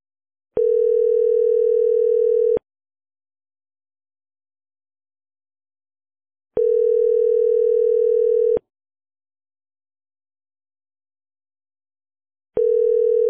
Ringback tone (North America), 440 + 480 Hz.